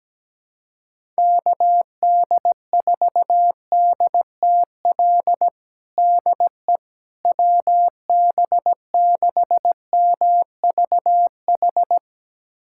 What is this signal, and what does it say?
Morse code, heard as a keyed tone, about 17 wpm, 700 Hz. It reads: KD4DTL DE WB6MVH
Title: USA Callsign DE Callsign - 6 Characters - 17wpm